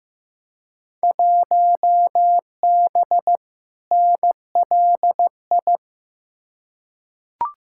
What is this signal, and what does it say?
Morse code, heard as a keyed tone, about 15 wpm, 700 Hz. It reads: 1B NLI
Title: Field Day Exchanges - 15wpm